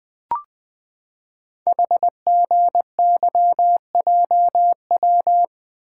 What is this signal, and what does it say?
Morse code, heard as a keyed tone, about 20 wpm, 700 Hz. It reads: HGYJW
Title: Code Groups - 5 Letters - 20wpm